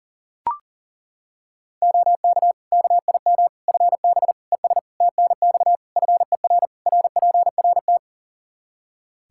Morse code, heard as a keyed tone, about 40 wpm, 700 Hz. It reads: OK KIM FB ES TNX FER RPRT
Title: Practice QSO - Exchange 2 - Info - 40wpm